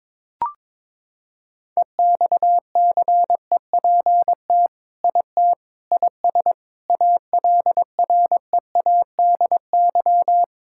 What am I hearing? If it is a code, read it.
Morse code, heard as a keyed tone, about 22 wpm, 700 Hz. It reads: EXCEPT IT IS ALREADY
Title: Sentences from Top 1600 Words 22wpm